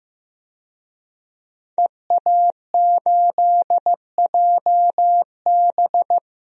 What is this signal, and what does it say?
Morse code, heard as a keyed tone, about 15 wpm, 700 Hz. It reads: EA8JB
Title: DXCC-100 Call Signs - 5 Characters - 15wpm